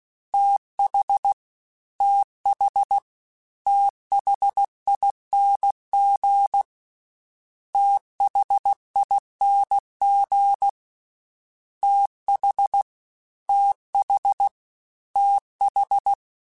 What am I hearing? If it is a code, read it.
Morse code, heard as a keyed tone, about 16 wpm, 800 Hz. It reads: TH TH THING THING TH TH TH
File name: THs / TH / TH Prefixes 3x2 @15